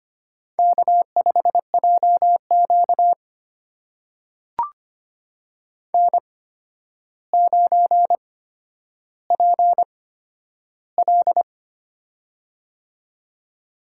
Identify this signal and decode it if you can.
Morse code, heard as a keyed tone, about 25 wpm, spaced as for 11 wpm, 700 Hz. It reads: K5JQ N9PL
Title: USA Call Signs - 4 Characters - 8x Character Spacing - 25wpm